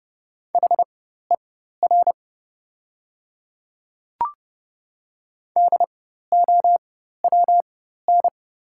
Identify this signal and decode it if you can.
Morse code, heard as a keyed tone, about 30 wpm, spaced as for 14 wpm, 700 Hz. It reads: HER DOWN
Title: 4x Character Spacing - Top 200 Words - 30wpm